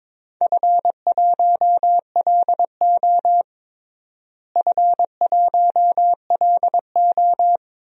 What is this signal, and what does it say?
Morse code, heard as a keyed tone, about 22 wpm, 700 Hz. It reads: F1LO F1LO
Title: DXCC-100 Call Signs - 4 Characters - Repeated 2x - 22wpm